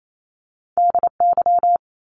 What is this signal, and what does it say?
Morse code, heard as a keyed tone, about 28 wpm, 700 Hz. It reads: DY
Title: Top Two Letter Combinations in Words 28wpm